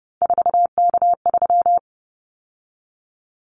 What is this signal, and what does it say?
Morse code, heard as a keyed tone, about 30 wpm, 700 Hz. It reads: 4K3
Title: Callsign Prefixes for Top 100 DXCC - 30wpm